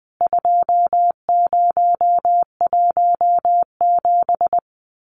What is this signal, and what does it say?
Morse code, heard as a keyed tone, about 20 wpm, 700 Hz. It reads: 2017
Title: CWT Exchange 20wpm